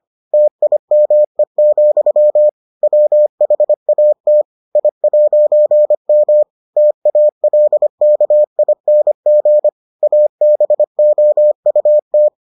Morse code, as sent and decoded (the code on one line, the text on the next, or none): - .. -- . --..-- / .-- .... .- - / .. .----. -- / - .- .-.. -.- .. -. --. / .- -... --- ..- -
TIME, WHAT I'M TALKING ABOUT